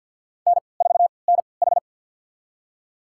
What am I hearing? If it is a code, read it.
Morse code, heard as a keyed tone, about 52 wpm, spaced as for 29 wpm, 700 Hz. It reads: N4NH
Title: USA Call Signs - 4 Characters - 3x Character Spacing - 50wpm